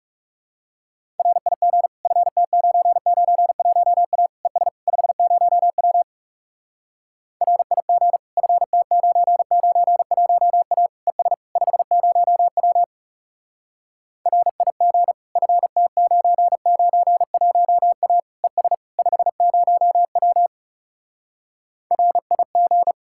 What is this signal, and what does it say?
Morse code, heard as a keyed tone, about 37 wpm, 700 Hz. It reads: RIG FT991A ES 50W RIG FT991A ES 50W RIG FT991A ES 50W RIG
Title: Speed-Racing Common QSO Phrases - 30wpm